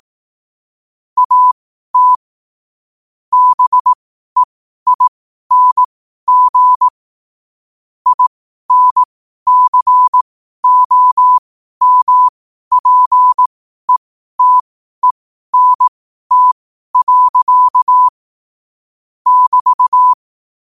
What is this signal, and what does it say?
Morse code, heard as a keyed tone, about 18 wpm, spaced as for 12 wpm, 1000 Hz. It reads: AT BEING INCOMPETENT. <BT>
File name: qotdfiles / 12wpm